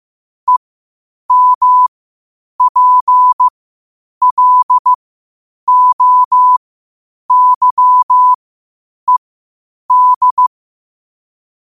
Morse code, heard as a keyed tone, about 15 wpm, spaced as for 8 wpm, 1000 Hz. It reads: EMPLOYED
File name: qotdfiles / 8wpm